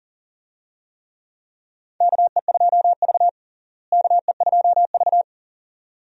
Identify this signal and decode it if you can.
Morse code, heard as a keyed tone, about 40 wpm, 700 Hz. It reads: KE2V KE2V
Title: USA Call Signs - 4 Characters - Repeated 2x - 40wpm